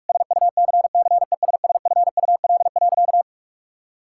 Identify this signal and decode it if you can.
Morse code, heard as a keyed tone, about 46 wpm, 700 Hz. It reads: SUCCESSFUL.
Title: Sentences from Top 1000 Words - Review Part 6 - 45wpm